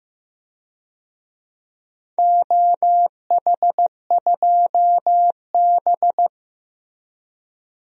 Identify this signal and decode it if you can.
Morse code, heard as a keyed tone, about 15 wpm, 700 Hz. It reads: OH2B
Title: DXCC-100 Call Signs - 4 Characters - Repeated 2x - 15wpm